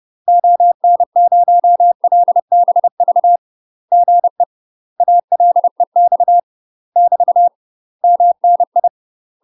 Morse code, heard as a keyed tone, about 30 wpm, 700 Hz. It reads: ON0LBV GE ALEX <BT> MNI
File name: UK MASTER QSOs TRAINING / 30WPM / UK MASTER QSOs 0039